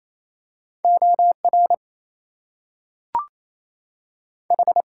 Morse code, heard as a keyed tone, about 27 wpm, 700 Hz. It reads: OR H
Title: Two Letter Words - 28wpm